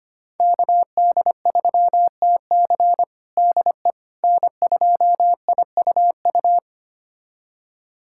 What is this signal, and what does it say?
Morse code, heard as a keyed tone, about 25 wpm, 700 Hz. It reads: KD3TC DE N2IUU
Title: USA Callsign DE Callsign - 5 Characters - 25wpm